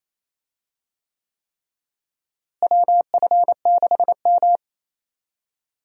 Morse code, heard as a keyed tone, about 28 wpm, 700 Hz. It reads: WF6M
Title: USA Call Signs - 4 Characters - Repeated 3x - 28wpm